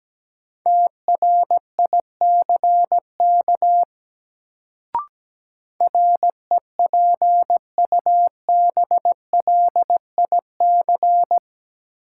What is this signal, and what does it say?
Morse code, heard as a keyed tone, about 17 wpm, 700 Hz. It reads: TRICK REPUBLIC
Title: Sentences from Top 1500 Words - New Word Drill - 17wpm